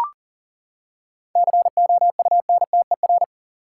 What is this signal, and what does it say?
Morse code, heard as a keyed tone, about 38 wpm, 700 Hz. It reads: COUNTER
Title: Seven Letter Words - 40wpm